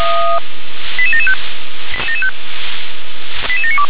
Five-tone selective call (ZVEI), heard as quick runs of five tones, tones of about 70 ms. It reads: 88775 88792
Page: SDR/ZVEI